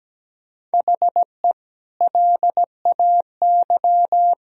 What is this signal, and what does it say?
Morse code, heard as a keyed tone, about 17 wpm, 700 Hz. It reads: HE LAY
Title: Sentences from Top 1000 Words - Review Part 2 - 17wpm